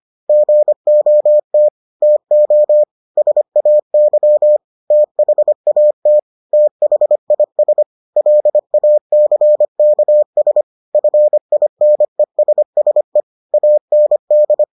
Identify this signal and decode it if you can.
Morse code, heard as a keyed tone, about 25 wpm, 600 Hz. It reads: GOT TO SAY THAT THIS LACKS FINESSE AND